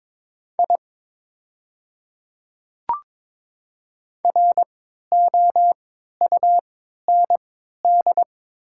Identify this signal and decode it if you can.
Morse code, heard as a keyed tone, about 22 wpm, spaced as for 12 wpm, 700 Hz. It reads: I ROUND
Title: 3x Character Spacing - Top 200 Words - 22wpm